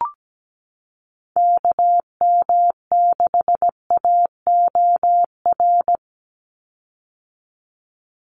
Morse code, heard as a keyed tone, about 17 wpm, 700 Hz. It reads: KM6AOR
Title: USA Call Signs 17wpm